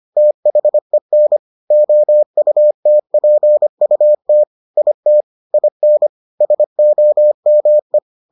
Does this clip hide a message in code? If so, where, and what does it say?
Yes: Morse code from 0.2 to 8.0 s, heard as a keyed tone, about 25 wpm, 600 Hz. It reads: THEN OUTPUT IT IN SOME